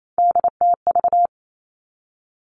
Morse code, heard as a keyed tone, about 28 wpm, 700 Hz. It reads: DTV
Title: Three Challenging Letter-Number Combinations 28wpm